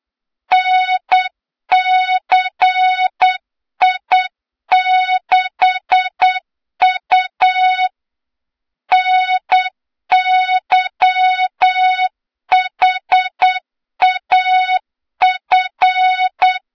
Morse code, heard as a keyed tone, about 8 wpm, 3700 Hz, with noise. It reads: NCI6U NYHAF
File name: Morze-HA2MN / hang / 23 GYAKORLAT